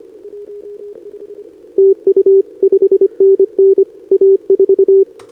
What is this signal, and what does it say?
Morse code, heard as a keyed tone, about 23 wpm, 410 Hz. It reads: TU 5NN A4
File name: CQWWCW2025 / NedeleOdpol 20m Scan